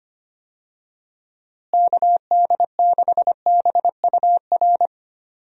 Morse code, heard as a keyed tone, about 25 wpm, 700 Hz. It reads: KD6BUR
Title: USA Call Signs - 6 Characters - Repeated 2x - 25wpm